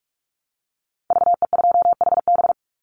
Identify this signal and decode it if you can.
Morse code, heard as a keyed tone, about 46 wpm, 700 Hz. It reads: VE2HB
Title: Call Signs - 45wpm